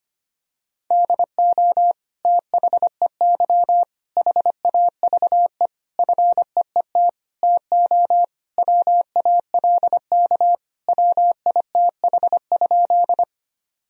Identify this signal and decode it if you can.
Morse code, heard as a keyed tone, about 25 wpm, 700 Hz. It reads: DO THEY HAVE FEET TO WALK WITH?
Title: Sentences from Top 1000 Words - Review Part 1 - 25wpm